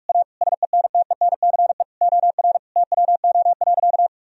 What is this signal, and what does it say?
Morse code, heard as a keyed tone, about 42 wpm, 700 Hz. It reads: A SENTENCE OR TWO.
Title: Sentences from Top 1000 Words - Review Part 1 - 45wpm